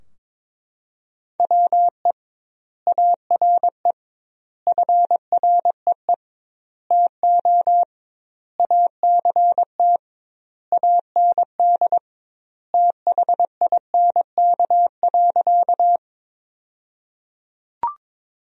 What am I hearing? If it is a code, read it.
Morse code, heard as a keyed tone, about 22 wpm, 700 Hz. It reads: WE ARE FREE TO ACT AND THINK.
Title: 2x Word-Spacing - Sentences from Top 500 Words - 22wpm